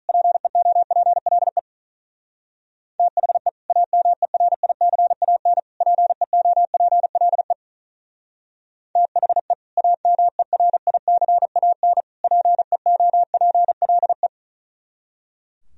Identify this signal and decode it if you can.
Morse code, heard as a keyed tone, about 44 wpm, 700 Hz. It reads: PEOPLE THE AMERICAN PEOPLE THE AMERICAN PEOPLE